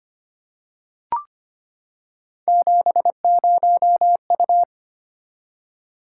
Morse code, heard as a keyed tone, about 25 wpm, 700 Hz. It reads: 70U
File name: Three Letter-Number Combinations - 25wpm